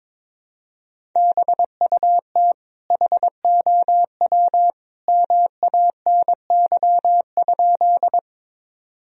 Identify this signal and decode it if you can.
Morse code, heard as a keyed tone, about 22 wpm, 700 Hz. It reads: BUT HOW MANY?
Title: Sentences from Top 1000 Words - Review Part 1 - 22wpm